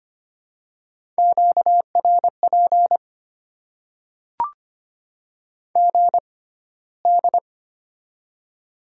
Morse code, heard as a keyed tone, about 25 wpm, spaced as for 15 wpm, 700 Hz. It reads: QRP GD
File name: Top 90 QSO Elements - 6x Character Spacing - 25wpm